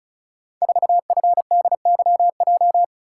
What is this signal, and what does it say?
Morse code, heard as a keyed tone, about 36 wpm, 700 Hz. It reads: 4FDYJ
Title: Code Groups - 5 Letter:Numbers - 35wpm